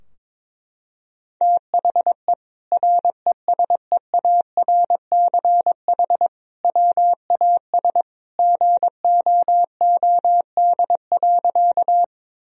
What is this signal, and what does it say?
Morse code, heard as a keyed tone, about 22 wpm, 700 Hz. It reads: THE RESEARCH WAS GOOD.
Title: Sentences from Top 1000 Words - Review Part 4 - 22wpm